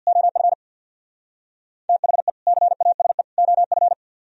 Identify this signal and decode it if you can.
Morse code, heard as a keyed tone, about 64 wpm, 700 Hz. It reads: OF THE CASE OF